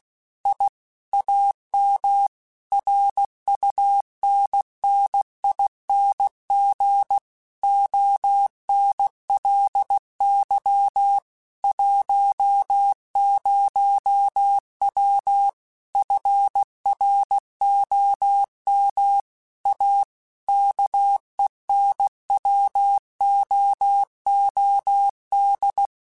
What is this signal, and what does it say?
Morse code, heard as a keyed tone, about 16 wpm, 800 Hz. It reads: I AM RUNNING ONLY 10W FROM A KENWOOD
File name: QSOs / QSOs-mp3s / WA7QOX 2@15